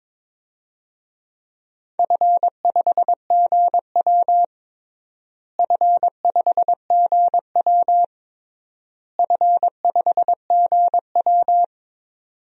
Morse code, heard as a keyed tone, about 22 wpm, 700 Hz. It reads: F5GW F5GW F5GW
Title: DXCC-100 Call Signs - 4 Characters - Repeated 3x - 22wpm